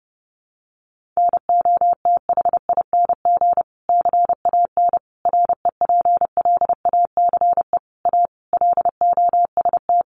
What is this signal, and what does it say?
Morse code, heard as a keyed tone, about 30 wpm, 700 Hz. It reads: NOTHING CAN REPLACE A LOST
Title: Sentences from Top 1400 Words 30wpm